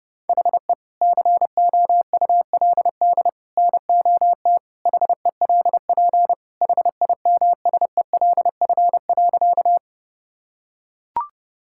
Morse code, heard as a keyed tone, about 30 wpm, 700 Hz. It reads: HE COULD NOT HELP HIMSELF.
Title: Sentences from Top 1600 Words 30wpm